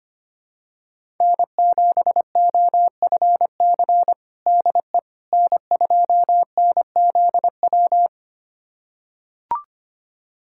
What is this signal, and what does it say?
Morse code, heard as a keyed tone, about 25 wpm, 700 Hz. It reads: N7OFC DE N2NZW
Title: USA Callsign DE Callsign - 5 Characters - 25wpm